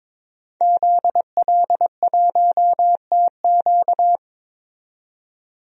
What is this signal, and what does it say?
Morse code, heard as a keyed tone, about 22 wpm, 700 Hz. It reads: ZL1TQ
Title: DXCC-100 Call Signs - 5 Characters - 22wpm